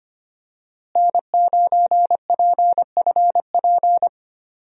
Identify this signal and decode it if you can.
Morse code, heard as a keyed tone, about 25 wpm, 700 Hz. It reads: N9PFP
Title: USA Call Signs - 5 Characters - Repeated 2x - 25wpm